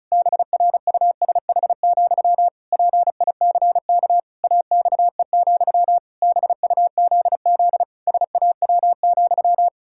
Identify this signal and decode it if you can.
Morse code, heard as a keyed tone, about 34 wpm, 700 Hz. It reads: BRUSH, PICK AXE, BUZZ SAW,
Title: Pangram Challenge v2 - 35wpm